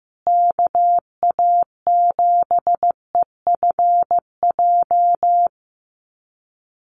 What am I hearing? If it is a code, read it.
Morse code, heard as a keyed tone, about 15 wpm, 700 Hz. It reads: KA7EFJ
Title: USA Call Signs Repeated 2x 15wpm